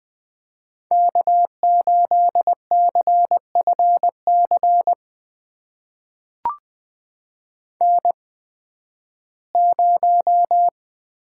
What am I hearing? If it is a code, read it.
Morse code, heard as a keyed tone, about 20 wpm, spaced as for 13 wpm, 700 Hz. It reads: K8CFC N0
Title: USA Call Signs - 5 Characters - 8x Character Spacing - 20wpm